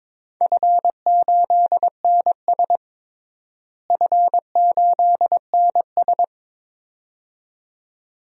Morse code, heard as a keyed tone, about 21 wpm, 700 Hz. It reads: F8NS F8NS
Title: DXCC-100 Call Signs - 4 Characters - Repeated 2x - 22wpm